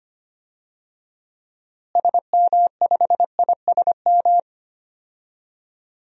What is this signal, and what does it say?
Morse code, heard as a keyed tone, about 25 wpm, 690 Hz. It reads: SM5ISM
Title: DXCC-100 Call Signs - 6 Characters - Repeated 3x - 25wpm